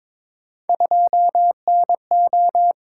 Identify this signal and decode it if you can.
Morse code, heard as a keyed tone, about 22 wpm, 700 Hz. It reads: 2NO
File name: practice / 22wpm / Three Letter-Number Combinations - 22wpm